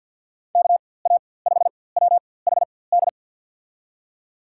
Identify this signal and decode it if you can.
Morse code, heard as a keyed tone, about 51 wpm, spaced as for 23 wpm, 700 Hz. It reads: KA5WHD
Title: USA Call Signs - 6 Characters - 4x Character Spacing - 50wpm